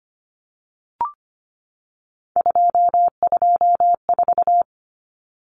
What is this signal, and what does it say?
Morse code, heard as a keyed tone, about 25 wpm, 700 Hz. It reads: 224
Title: Common Cut Numbers - 3-Digit 25wpm